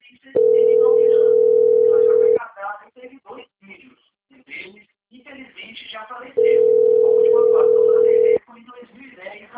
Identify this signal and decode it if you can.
Ringback tone (North America), 440 + 480 Hz, with speech.
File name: out- - - - - .19604